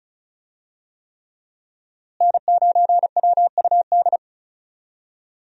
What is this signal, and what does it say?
Morse code, heard as a keyed tone, about 35 wpm, 700 Hz. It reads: N9WUD
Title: USA Call Signs - 5 Characters - Repeated 3x - 35wpm